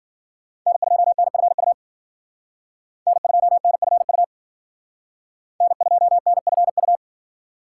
Morse code, heard as a keyed tone, about 50 wpm, 700 Hz. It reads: N2NFU N2NFU N2NFU
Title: Speed Racing - USA Call Signs 40wpm